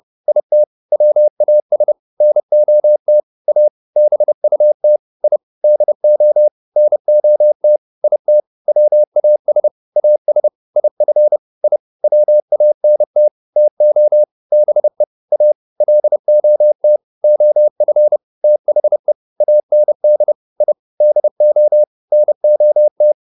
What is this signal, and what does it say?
Morse code, heard as a keyed tone, about 30 wpm, 600 Hz. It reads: IT WAS NOT A BUT I DO NOT IT WAS AS IF I WANT TO BE A LOT OF THE AND I DO NOT